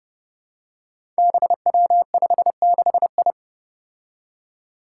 Morse code, heard as a keyed tone, about 30 wpm, 700 Hz. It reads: BW56I
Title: Five Challenging Letter-Number Combinations 30wpm